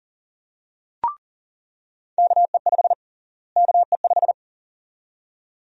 Morse code, heard as a keyed tone, about 40 wpm, 700 Hz. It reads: KE5 KE5
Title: USA Call Sign Prefixes - Repeated 2x - 40wpm